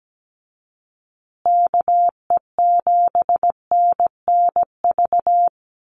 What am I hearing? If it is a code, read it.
Morse code, heard as a keyed tone, about 17 wpm, 700 Hz. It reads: KE7NNV
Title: USA Call Signs Repeated 2x 17wpm